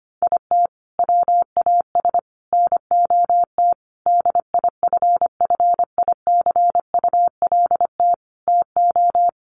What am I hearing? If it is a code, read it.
Morse code, heard as a keyed tone, about 25 wpm, 700 Hz. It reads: IT WAS NOT DIFFICULT TO
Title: Sentences from Top 1500 Words 25wpm